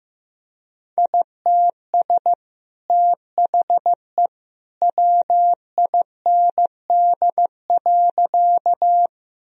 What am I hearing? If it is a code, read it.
Morse code, heard as a keyed tone, about 15 wpm, 700 Hz. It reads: ITS THE WIND.
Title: Sentences from Top 1000 Words - Review Part 2 - 15wpm